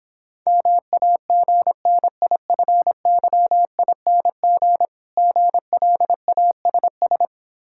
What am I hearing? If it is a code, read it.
Morse code, heard as a keyed tone, about 26 wpm, 690 Hz. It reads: MAGNIFYING GLASS